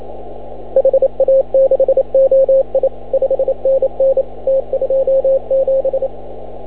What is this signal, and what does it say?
Morse code, heard as a keyed tone, about 28 wpm, 540 Hz. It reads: HA6OI 5NN T27
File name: CW / ha6oi 2